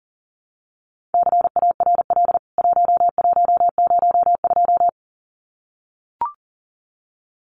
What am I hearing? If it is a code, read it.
Morse code, heard as a keyed tone, about 40 wpm, 700 Hz. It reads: CARL 1102
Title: CWT Exchange 40wpm